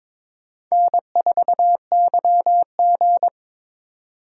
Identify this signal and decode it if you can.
Morse code, heard as a keyed tone, about 22 wpm, 700 Hz. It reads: N4YG
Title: USA Call Signs - 4 Characters - 2x Character Spacing - 22wpm